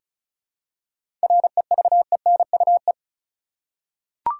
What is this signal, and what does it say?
Morse code, heard as a keyed tone, about 36 wpm, 700 Hz. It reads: REVENUE
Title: Seven Letter Words - 35wpm